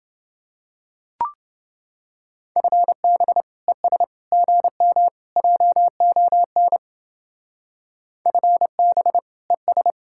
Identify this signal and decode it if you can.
Morse code, heard as a keyed tone, about 28 wpm, 700 Hz. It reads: FB ES GM JON FB ES